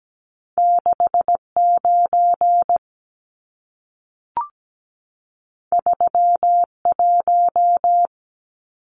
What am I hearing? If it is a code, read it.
Morse code, heard as a keyed tone, about 17 wpm, 700 Hz. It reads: 69 31
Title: Numbers - 2-Digit 17wpm